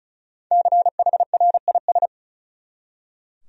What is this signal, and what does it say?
Morse code, heard as a keyed tone, about 36 wpm, 700 Hz. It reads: CHRIS
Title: USA Common Names - 35wpm